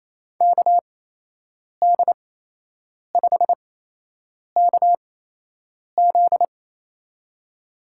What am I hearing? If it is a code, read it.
Morse code, heard as a keyed tone, about 28 wpm, spaced as for 8 wpm, 700 Hz. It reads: KD5KZ
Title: USA Call Signs - 5 Characters - 8x Character Spacing - 28wpm